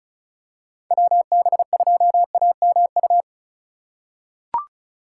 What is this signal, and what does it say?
Morse code, heard as a keyed tone, about 36 wpm, 700 Hz. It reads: WB2AMU
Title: USA Call Signs - 6 Characters - 3x Character Spacing - 35wpm